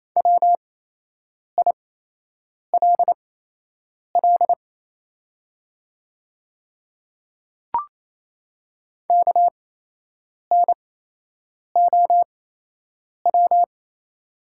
Morse code, heard as a keyed tone, about 28 wpm, spaced as for 8 wpm, 700 Hz. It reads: WILL KNOW
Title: 8x Character Spacing - Top 100 Words - 28wpm